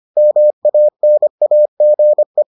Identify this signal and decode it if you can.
Morse code, heard as a keyed tone, about 25 wpm, 600 Hz. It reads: MANAGE